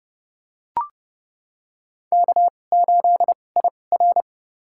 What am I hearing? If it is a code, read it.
Morse code, heard as a keyed tone, about 32 wpm, spaced as for 22 wpm, 700 Hz. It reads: K8IR
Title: USA Call Signs - 4 Characters - 2x Character Spacing - 30wpm